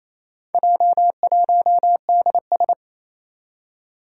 Morse code, heard as a keyed tone, about 27 wpm, 700 Hz. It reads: J1DS
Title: Four Challenging Letter-Number Combinations 28wpm